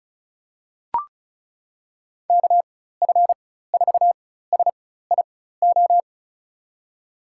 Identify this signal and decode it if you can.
Morse code, heard as a keyed tone, about 36 wpm, spaced as for 16 wpm, 700 Hz. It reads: KF4SIO
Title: USA Call Signs - 4x Character Spacing - 35wpm